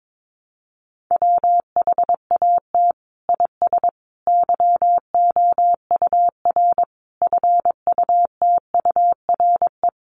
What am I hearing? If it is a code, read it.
Morse code, heard as a keyed tone, about 22 wpm, 700 Hz. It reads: WHAT IS YOUR FUTURE
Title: Sentences from Top 1000 Words 22wpm